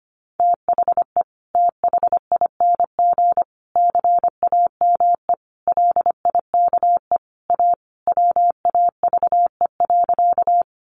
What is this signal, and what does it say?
Morse code, heard as a keyed tone, about 25 wpm, 700 Hz. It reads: THE THING CAME LIKE A WAVE.
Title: Sentences from Top 1400 Words 25wpm